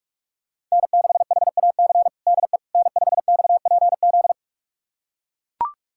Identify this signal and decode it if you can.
Morse code, heard as a keyed tone, about 42 wpm, 700 Hz. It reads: N6HAC DE N5XPZ
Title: USA Callsign DE Callsign - 5 Characters - 45wpm